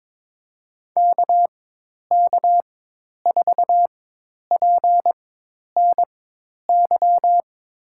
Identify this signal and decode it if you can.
Morse code, heard as a keyed tone, about 22 wpm, spaced as for 10 wpm, 700 Hz. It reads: KK4PNY
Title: USA Call Signs - 4x Character Spacing - 22wpm